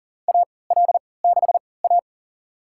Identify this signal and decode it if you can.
Morse code, heard as a keyed tone, about 38 wpm, spaced as for 22 wpm, 700 Hz. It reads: AL6A
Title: USA Call Signs - 4 Characters - 3x Character Spacing - 40wpm